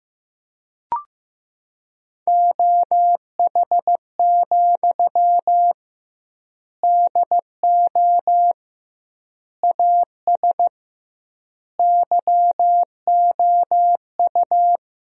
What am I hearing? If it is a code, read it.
Morse code, heard as a keyed tone, about 15 wpm, 700 Hz. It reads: OH, DO AS YOU
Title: 2x Word-Spacing - Sentences from Top 500 Words - 15wpm